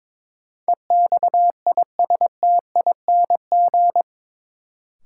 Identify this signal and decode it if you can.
Morse code, heard as a keyed tone, about 22 wpm, 700 Hz. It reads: EXISTING
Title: Sentences from Top 1500 Words - New Word Drill - 22wpm